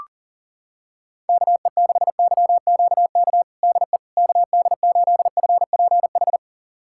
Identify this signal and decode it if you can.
Morse code, heard as a keyed tone, about 40 wpm, 700 Hz. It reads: KE6YQK DE KD8FPH
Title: USA Callsign DE Callsign 40wpm